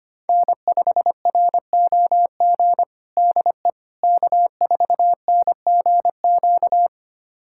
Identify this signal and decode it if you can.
Morse code, heard as a keyed tone, about 25 wpm, 700 Hz. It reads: N5ROG DE K4NGQ
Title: USA Callsign DE Callsign - 5 Characters - 25wpm